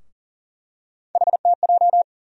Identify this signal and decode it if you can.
Morse code, heard as a keyed tone, about 38 wpm, 700 Hz. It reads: HTJ
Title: USA Call Sign Postfixes - 3 Letters 40wpm